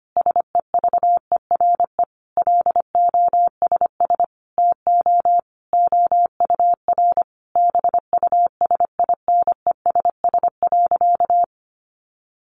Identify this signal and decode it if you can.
Morse code, heard as a keyed tone, about 25 wpm, 700 Hz. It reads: SEVERE LOSS TO OUR BUSINESS.
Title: Sentences from Top 1400 Words 25wpm